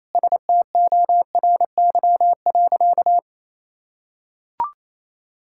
Morse code, heard as a keyed tone, about 28 wpm, 700 Hz. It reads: STORY.
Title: Sentences from Top 1000 Words - Review Part 6 - 28wpm